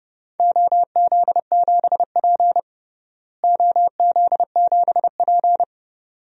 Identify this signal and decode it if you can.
Morse code, heard as a keyed tone, about 30 wpm, 700 Hz. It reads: OZ7P OZ7P
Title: DXCC-100 Call Signs - 4 Characters - Repeated 2x - 30wpm